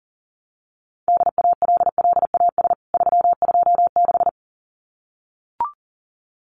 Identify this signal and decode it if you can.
Morse code, heard as a keyed tone, about 40 wpm, 700 Hz. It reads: DALLAS 326
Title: CWT Exchange 40wpm